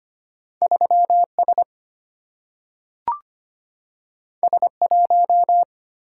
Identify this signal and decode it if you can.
Morse code, heard as a keyed tone, about 25 wpm, 700 Hz. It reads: 3S S1
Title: Single-Letter Single-Number 25wpm